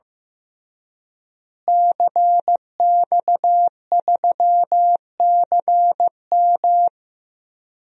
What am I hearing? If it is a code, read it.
Morse code, heard as a keyed tone, about 15 wpm, 700 Hz. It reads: CX3CM
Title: DXCC-100 Call Signs - 5 Characters - Repeated 3x - 15wpm